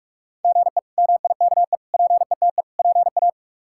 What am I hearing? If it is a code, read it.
Morse code, heard as a keyed tone, about 41 wpm, 700 Hz. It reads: GE MIKE PETE PA
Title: K1USN SST QSO Practice - 45wpm